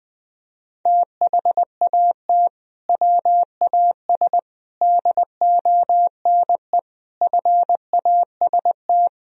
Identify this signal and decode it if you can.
Morse code, heard as a keyed tone, about 20 wpm, 700 Hz. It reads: THAT WAS DONE FAST
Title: Sentences from Top 1000 Words - Review Part 2 - 20wpm